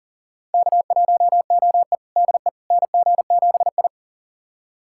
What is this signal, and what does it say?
Morse code, heard as a keyed tone, about 37 wpm, 700 Hz. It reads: K1OE DE NG7I
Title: USA Callsign DE Callsign - 4 Characters - 40wpm